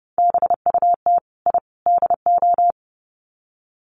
Morse code, heard as a keyed tone, about 30 wpm, 700 Hz. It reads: BUT I DO
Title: Sets of 3 Words 30wpm